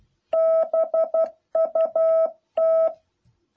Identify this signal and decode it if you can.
Morse code, heard as a keyed tone, about 12 wpm, 660 Hz. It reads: BUT
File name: jh8cbh / 2021cw